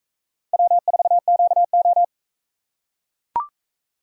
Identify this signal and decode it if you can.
Morse code, heard as a keyed tone, about 40 wpm, 700 Hz. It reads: W4QO